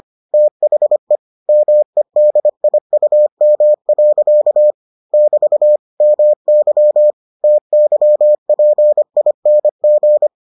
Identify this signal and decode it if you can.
Morse code, heard as a keyed tone, about 25 wpm, 600 Hz. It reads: THE MEDIUM. <BT> MY TYPING